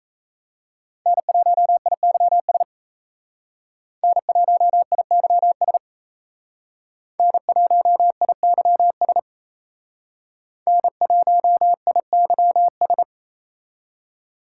Morse code, heard as a keyed tone, about 35 wpm, 700 Hz. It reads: N1IYS N1IYS N1IYS N1IYS